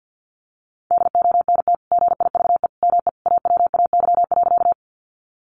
Morse code, heard as a keyed tone, about 50 wpm, 700 Hz. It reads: DONT GIVE ME AWAY.
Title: Sentences from Top 600 Words 50wpm